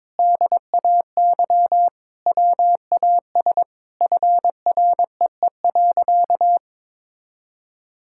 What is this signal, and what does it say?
Morse code, heard as a keyed tone, about 22 wpm, 700 Hz. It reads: DAY WAS FREE.
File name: practice / 22wpm / Sentences with 4-Word Phrases - 22wpm